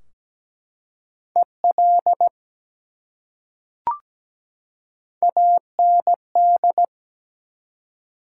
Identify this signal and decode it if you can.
Morse code, heard as a keyed tone, about 17 wpm, 700 Hz. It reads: EL AND
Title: Top 180 QSO Elements 17wpm